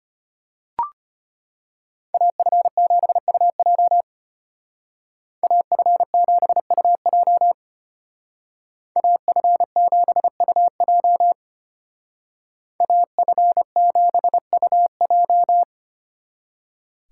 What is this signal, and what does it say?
Morse code, heard as a keyed tone, about 31 wpm, 700 Hz. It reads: AF7UJ AF7UJ AF7UJ AF7UJ
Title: Speed Racing - USA 2x2 Call Signs 25wpm